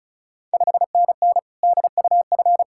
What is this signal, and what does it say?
Morse code, heard as a keyed tone, about 36 wpm, 690 Hz. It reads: 5NN DUF
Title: Ontario QSO Party Exchange 35wpm